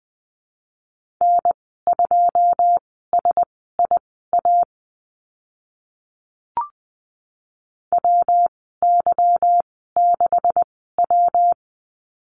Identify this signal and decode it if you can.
Morse code, heard as a keyed tone, about 20 wpm, spaced as for 15 wpm, 700 Hz. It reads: N2SIA WY6W
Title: 161 Lesson 26g - Letter g - Call Signs 20wpm